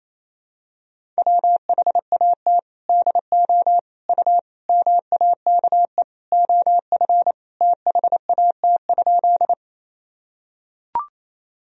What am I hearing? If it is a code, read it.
Morse code, heard as a keyed tone, about 28 wpm, 700 Hz. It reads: WHAT DO U MAKE OF THAT?
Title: Sentences from Top 100 Words and Abbreviations 28wpm